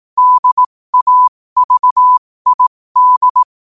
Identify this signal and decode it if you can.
Morse code, heard as a keyed tone, about 18 wpm, spaced as for 15 wpm, 1000 Hz. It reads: DAVID
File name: qotdfiles / 15wpm